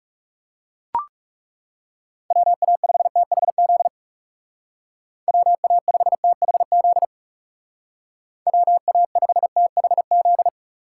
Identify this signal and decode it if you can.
Morse code, heard as a keyed tone, about 37 wpm, 700 Hz. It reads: WA5THZ WA5THZ WA5THZ
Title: Speed Racing - USA 2x3 Call Signs 30wpm